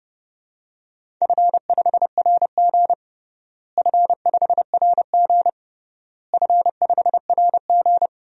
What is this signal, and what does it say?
Morse code, heard as a keyed tone, about 30 wpm, 700 Hz. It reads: F5RG F5RG F5RG
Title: DXCC-100 Call Signs - 4 Characters - Repeated 3x - 30wpm